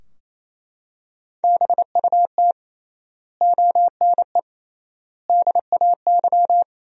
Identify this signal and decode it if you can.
Morse code, heard as a keyed tone, about 28 wpm, 700 Hz. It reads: BUT ONE DAY
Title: 3x Word-Spacing - Sentences from Top 400 Words - 28wpm